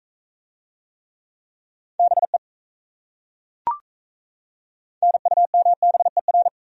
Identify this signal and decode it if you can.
Morse code, heard as a keyed tone, about 37 wpm, 700 Hz. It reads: BE NUMBER